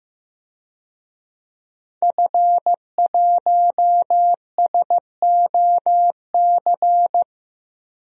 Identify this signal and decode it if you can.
Morse code, heard as a keyed tone, about 16 wpm, 700 Hz. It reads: F1SOC
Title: DXCC-100 Call Signs - 5 Characters - Repeated 2x - 15wpm